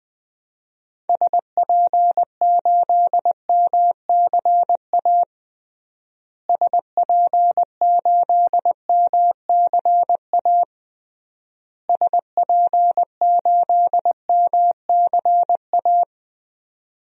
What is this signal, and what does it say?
Morse code, heard as a keyed tone, about 20 wpm, 700 Hz. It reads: SP8MCA SP8MCA SP8MCA
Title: DXCC-100 Call Signs - 6 Characters - Repeated 3x - 20wpm